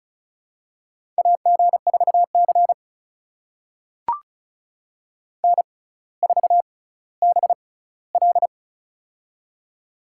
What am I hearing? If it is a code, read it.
Morse code, heard as a keyed tone, about 36 wpm, spaced as for 18 wpm, 700 Hz. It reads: AG4C N4BL
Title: USA Call Signs - 4 Characters - 6x Character Spacing - 35wpm